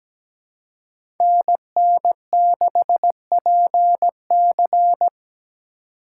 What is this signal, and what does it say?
Morse code, heard as a keyed tone, about 17 wpm, 700 Hz. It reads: NN6PC
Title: USA Call Signs - 5 Characters - 4x Character Spacing - 17wpm